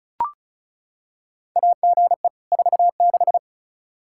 Morse code, heard as a keyed tone, about 36 wpm, 700 Hz. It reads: AGE 46
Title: Practice QSO - Exchange 3 - More Info - 35wpm